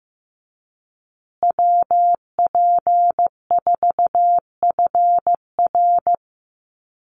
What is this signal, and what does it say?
Morse code, heard as a keyed tone, about 15 wpm, 700 Hz. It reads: WP4FR
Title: USA Call Signs Repeated 2x 15wpm